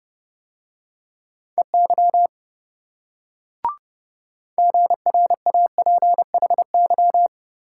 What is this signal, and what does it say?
Morse code, heard as a keyed tone, about 27 wpm, 700 Hz. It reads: EY GRAPHY